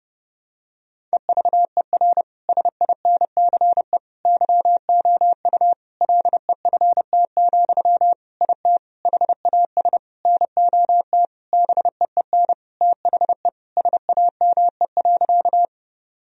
Morse code, heard as a keyed tone, about 30 wpm, 700 Hz. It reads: EVER SINCE YOU LEFT, IT HAS NOT BEEN THE SAME.
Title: Sentences from Top 1000 Words - Review Part 1 - 30wpm